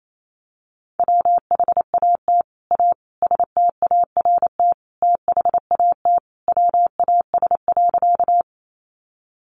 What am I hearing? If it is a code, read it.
Morse code, heard as a keyed tone, about 28 wpm, 700 Hz. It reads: WHAT A START THAT WAS.
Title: Sentences from Top 300 Words 28wpm